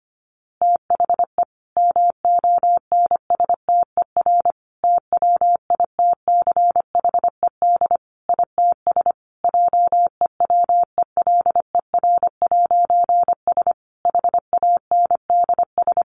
Morse code, heard as a keyed tone, about 25 wpm, 700 Hz. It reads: THE MONSTER TWITCHED ITS JEWELER'S HANDS
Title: A Sound Of Thunder 25wpm